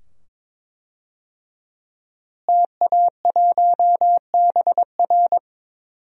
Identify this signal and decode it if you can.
Morse code, heard as a keyed tone, about 22 wpm, 700 Hz. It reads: TA1BR
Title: DXCC-100 Call Signs - 5 Characters - Repeated 2x - 22wpm